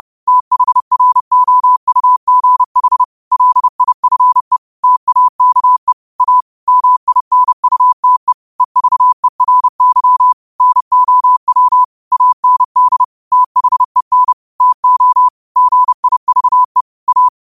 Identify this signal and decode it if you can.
Morse code, heard as a keyed tone, about 30 wpm, 1000 Hz. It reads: THROUGH LIFE TAKE A MINUTE EVERY NOW AND THEN TO GIVE A